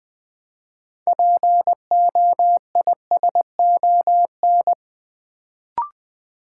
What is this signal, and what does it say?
Morse code, heard as a keyed tone, about 20 wpm, 700 Hz. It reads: POISON
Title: Build A Word - 3 Letters At A Time - 6-Letter Words - 20wpm